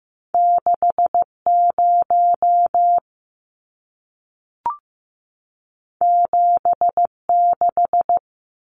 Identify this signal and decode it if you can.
Morse code, heard as a keyed tone, about 15 wpm, 700 Hz. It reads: 6 0 7 6
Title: Numbers - 2-Digit 15wpm